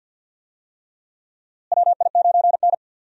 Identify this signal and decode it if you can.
Morse code, heard as a keyed tone, about 52 wpm, 700 Hz. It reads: WI9N
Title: USA Call Signs - 4 Characters - Repeated 3x - 50wpm